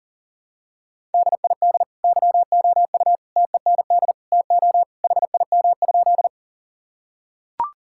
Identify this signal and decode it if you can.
Morse code, heard as a keyed tone, about 37 wpm, 700 Hz. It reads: DID YOU TEND TO HIM?
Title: Sentences from Top 1000 Words - Review Part 5 - 40wpm